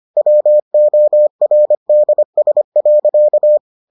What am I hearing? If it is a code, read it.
Morse code, heard as a keyed tone, about 25 wpm, 600 Hz. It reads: WORDS.